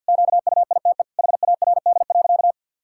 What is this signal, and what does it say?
Morse code, heard as a keyed tone, about 52 wpm, 700 Hz. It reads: QUITE HARD.
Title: Pangram Challenge v2 - 50wpm